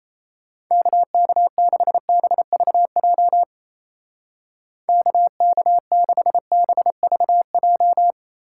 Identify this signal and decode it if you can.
Morse code, heard as a keyed tone, about 36 wpm, 700 Hz. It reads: KK6BVJ KK6BVJ